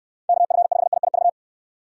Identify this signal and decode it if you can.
Morse code, heard as a keyed tone, about 69 wpm, 700 Hz. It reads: KF5IIV